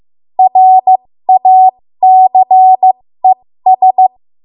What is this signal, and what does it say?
Morse code, heard as a keyed tone, about 15 wpm, spaced as for 13 wpm, 750 Hz. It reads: RACES
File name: Morse / 13WPM